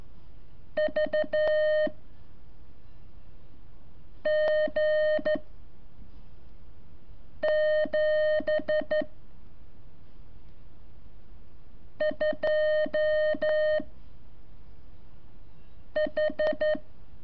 Morse code, heard as a keyed tone, about 12 wpm, 610 Hz. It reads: V G 7 2 H
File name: curso / ejercicio 24